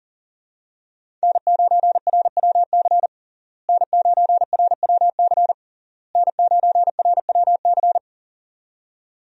Morse code, heard as a keyed tone, about 40 wpm, 700 Hz. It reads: N9RWC N9RWC N9RWC
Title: USA Call Signs - 5 Characters - Repeated 3x - 40wpm